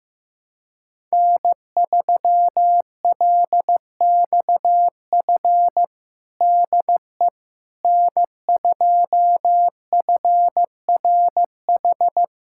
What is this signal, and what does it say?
Morse code, heard as a keyed tone, about 14 wpm, 700 Hz. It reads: N3LXF DE N2FRH
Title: USA Callsign DE Callsign - 5 Characters - 15wpm